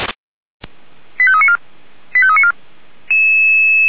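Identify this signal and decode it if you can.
Five-tone selective call (ZVEI), heard as quick runs of five tones, tones of about 70 ms. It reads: 85394 85394
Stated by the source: Received on an SDR, ZVEI five-tone selective call.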